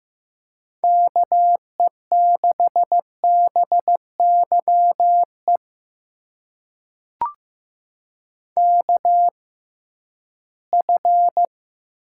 Morse code, heard as a keyed tone, about 15 wpm, spaced as for 11 wpm, 700 Hz. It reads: KE6BYE KF
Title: USA Call Signs - 6 Characters - 6x Character Spacing - 15wpm